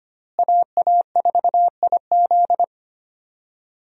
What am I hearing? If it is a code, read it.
Morse code, heard as a keyed tone, about 25 wpm, 700 Hz. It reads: AA4IZ
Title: USA Call Signs - 5 Characters - Repeated 3x - 25wpm